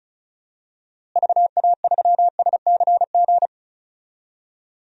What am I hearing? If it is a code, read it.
Morse code, heard as a keyed tone, about 34 wpm, 700 Hz. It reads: VA3SCG
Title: DXCC-100 Call Signs - 6 Characters - Repeated 2x - 35wpm